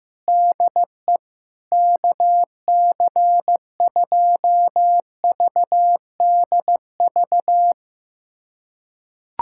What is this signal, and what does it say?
Morse code, heard as a keyed tone, about 15 wpm, 700 Hz. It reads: DE KC2VDV
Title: USA Callsign DE Callsign - 6 Characters - 15wpm